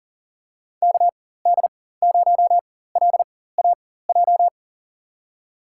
Morse code, heard as a keyed tone, about 38 wpm, spaced as for 18 wpm, 700 Hz. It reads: KD0LAJ
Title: USA Call Signs - 6 Characters - 4x Character Spacing - 40wpm